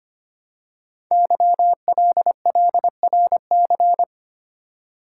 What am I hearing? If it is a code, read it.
Morse code, heard as a keyed tone, about 25 wpm, 700 Hz. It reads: YLLRC
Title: Code Groups - 5 Letters - 25wpm